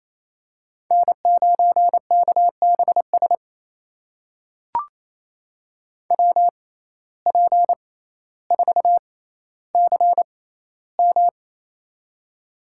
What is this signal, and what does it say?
Morse code, heard as a keyed tone, about 28 wpm, spaced as for 14 wpm, 700 Hz. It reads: N9KBS WP4CM
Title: USA Call Signs - 5 Characters - 6x Character Spacing - 28wpm